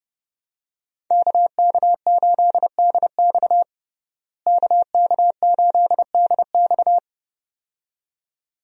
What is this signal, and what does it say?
Morse code, heard as a keyed tone, about 32 wpm, 700 Hz. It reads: KK8DX KK8DX
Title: USA Call Signs - 5 Characters - Repeated 2x - 30wpm